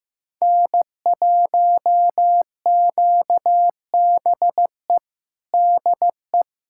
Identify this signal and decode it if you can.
Morse code, heard as a keyed tone, about 15 wpm, 700 Hz. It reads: N1QBE DE
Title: USA Callsign DE Callsign - 5 Characters - 15wpm